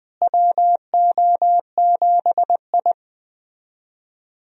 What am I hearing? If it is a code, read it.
Morse code, heard as a keyed tone, about 20 wpm, 700 Hz. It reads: WO7I
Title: USA Call Signs - 4 Characters - Repeated 3x - 20wpm